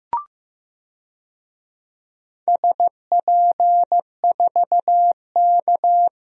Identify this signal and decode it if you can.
Morse code, heard as a keyed tone, about 15 wpm, 700 Hz. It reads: SP4K
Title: DXCC-100 Call Signs - 4 Characters - 15wpm